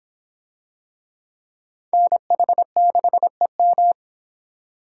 Morse code, heard as a keyed tone, about 24 wpm, 700 Hz. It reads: NH6EM